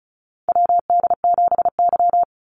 Morse code, heard as a keyed tone, about 34 wpm, 700 Hz. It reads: WD7Y
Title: NA SOTA Chaser Callsigns 35wpm